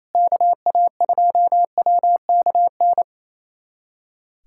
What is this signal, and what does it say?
Morse code, heard as a keyed tone, about 27 wpm, 700 Hz. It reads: KA2WKN
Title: USA Callsign DE Callsign 28wpm